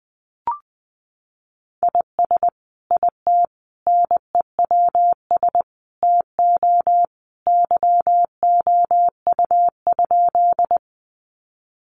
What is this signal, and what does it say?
Morse code, heard as a keyed tone, about 20 wpm, 700 Hz. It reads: IS IT NEWS TO YOU?
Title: Sentences from Top 800 Words 20wpm